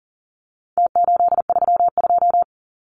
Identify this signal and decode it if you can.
Morse code, heard as a keyed tone, about 40 wpm, 700 Hz. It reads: T832
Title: Numbers - Contest Style - 4-Digit 40wpm